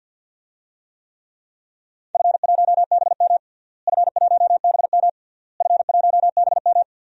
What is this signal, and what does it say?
Morse code, heard as a keyed tone, about 46 wpm, 700 Hz. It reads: F1BM F1BM F1BM
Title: DXCC-100 Call Signs - 4 Characters - Repeated 3x - 50wpm